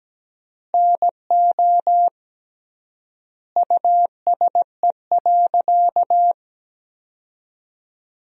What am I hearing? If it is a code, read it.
Morse code, heard as a keyed tone, about 17 wpm, 700 Hz. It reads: NO USE.
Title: 3x Word-Spacing - Sentences from Top 100 Words - 17wpm